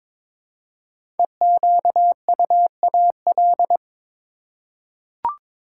Morse code, heard as a keyed tone, about 22 wpm, 700 Hz. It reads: EQUAL
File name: 5 Letter Words - First 3 Letters Given - 22wpm